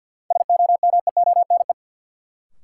Morse code, heard as a keyed tone, about 52 wpm, 700 Hz. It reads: SOMEONE
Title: Sentences from Top 600 Words - New Word Drill - 50wpm